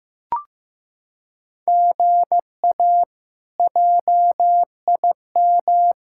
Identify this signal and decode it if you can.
Morse code, heard as a keyed tone, about 15 wpm, 700 Hz. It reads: GA JIM
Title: K1USN SST QSO Practice - 15wpm